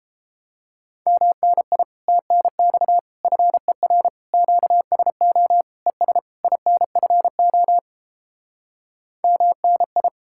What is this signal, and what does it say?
Morse code, heard as a keyed tone, about 29 wpm, 700 Hz. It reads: MNI TNX FER QSO ES INFO MNI